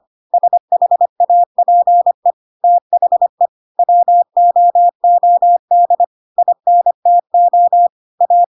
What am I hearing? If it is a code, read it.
Morse code, heard as a keyed tone, about 25 wpm, 700 Hz. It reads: SHAPE THE WOOD INTO A
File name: Pinocchio / Pinocchio ch3 25 25 ik1hge